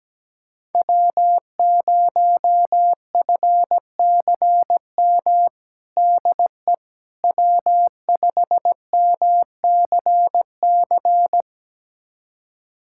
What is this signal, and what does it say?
Morse code, heard as a keyed tone, about 17 wpm, 700 Hz. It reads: W0FCM DE W5MCC
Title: USA Callsign DE Callsign - 5 Characters - 17wpm